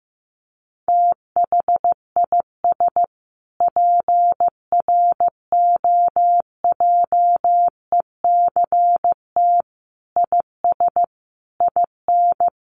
Morse code, heard as a keyed tone, about 15 wpm, 700 Hz. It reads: THIS PROJECT IS IN
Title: Sentences from Top 1400 Words 15wpm